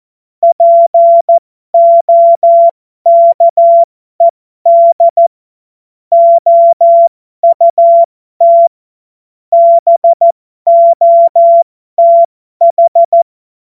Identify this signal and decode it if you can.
Morse code, heard as a keyed tone, about 14 wpm, spaced as for 12 wpm, 680 Hz. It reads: POKED OUT BOTH